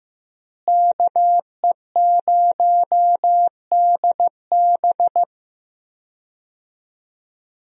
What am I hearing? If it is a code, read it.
Morse code, heard as a keyed tone, about 15 wpm, 700 Hz. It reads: KE0DB
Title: USA Call Signs - 5 Characters - Repeated 3x - 15wpm